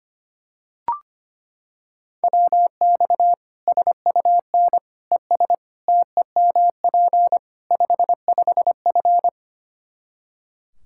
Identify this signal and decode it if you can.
Morse code, heard as a keyed tone, about 25 wpm, 700 Hz. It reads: WX SUN ES TEMP 55F
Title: Practice QSO - Exchange 2 - Info - 25wpm